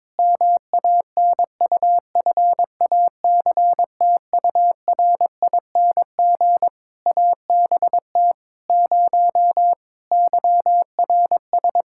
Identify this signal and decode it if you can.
Morse code, heard as a keyed tone, about 22 wpm, 700 Hz. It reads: MANUFACTURING ABT 0 YRS
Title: Practice QSO - Exchange 3 - More Info - 22wpm